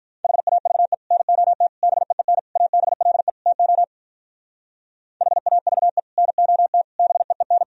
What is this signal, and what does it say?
Morse code, heard as a keyed tone, about 46 wpm, 700 Hz. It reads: HAVE NOT BEEN ABLE TO HAVE NOT BEEN